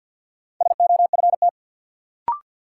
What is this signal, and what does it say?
Morse code, heard as a keyed tone, about 53 wpm, 700 Hz. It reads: SOFT
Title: Sentences from Top 1500 Words - New Word Drill Alt - 50wpm